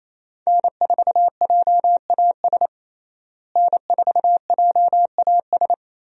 Morse code, heard as a keyed tone, about 27 wpm, 700 Hz. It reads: N4JAS N4JAS
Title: USA Call Signs - 5 Characters - Repeated 3x - 28wpm